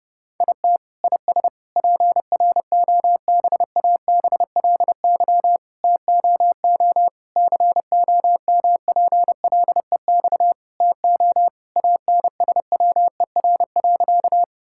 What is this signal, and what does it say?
Morse code, heard as a keyed tone, about 30 wpm, 700 Hz. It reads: IT IS PROBABLY TOO COMPLEX TO ANSWER.
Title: Sentences from Top 1000 Words - Review Part 6 - 30wpm